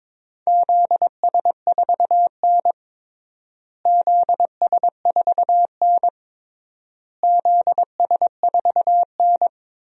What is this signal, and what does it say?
Morse code, heard as a keyed tone, about 22 wpm, 700 Hz. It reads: ZS4N ZS4N ZS4N
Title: DXCC-100 Call Signs - 4 Characters - Repeated 3x - 22wpm